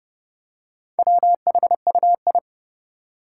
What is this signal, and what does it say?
Morse code, heard as a keyed tone, about 30 wpm, 700 Hz. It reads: WHUI
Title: Four Challenging Letter-Number Combinations 30wpm